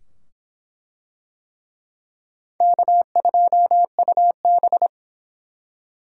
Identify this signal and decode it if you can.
Morse code, heard as a keyed tone, about 24 wpm, 700 Hz. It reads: K2UB